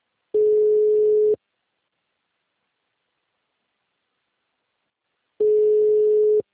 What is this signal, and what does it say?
Ringback tone (Europe, 425 Hz).
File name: out- -2013- - - .40105